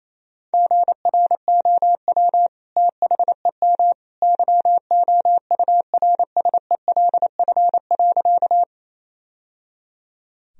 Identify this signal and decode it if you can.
Morse code, heard as a keyed tone, about 28 wpm, 700 Hz. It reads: GROW THEM YOURSELF.
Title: Sentences from Top 1000 Words - Review Part 1 - 28wpm